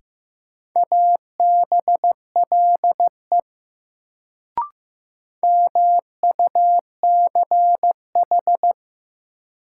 Morse code, heard as a keyed tone, about 15 wpm, 700 Hz. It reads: ABLE MUCH
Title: Sets of 2 Words - New Word Drill - 15wpm